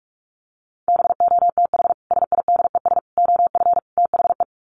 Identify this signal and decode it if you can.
Morse code, heard as a keyed tone, about 44 wpm, 700 Hz. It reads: BOTH SIDES OF THE
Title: Sets of 4 Words 45wpm